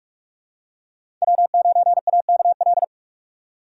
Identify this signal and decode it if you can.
Morse code, heard as a keyed tone, about 42 wpm, 700 Hz. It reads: W9AKL
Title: USA Call Signs - 5 Characters - 2x Character Spacing - 45wpm